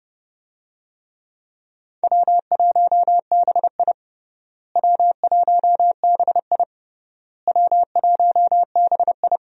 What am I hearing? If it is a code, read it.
Morse code, heard as a keyed tone, about 32 wpm, 700 Hz. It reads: W1BI W1BI W1BI
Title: USA Call Signs - 4 Characters - Repeated 3x - 30wpm